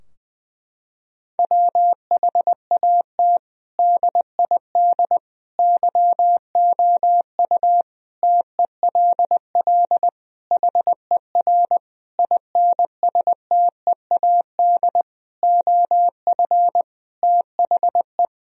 Morse code, heard as a keyed tone, about 21 wpm, 700 Hz. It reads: WHAT DID YOU TELL HER INSTEAD OF THE
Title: Sentences from Top 1000 Words - Review Part 4 - 20wpm